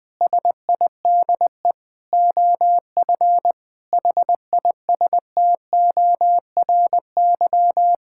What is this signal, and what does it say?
Morse code, heard as a keyed tone, about 20 wpm, 700 Hz. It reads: SIDE OF HISTORY
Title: Sentences with 8-Word Phrases - 20wpm